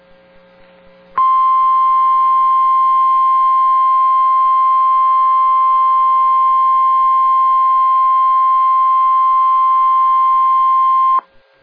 Emergency Alert System (SAME) tones (the 1050 Hz attention tone). It is heard from 1.1 to 11.2 s.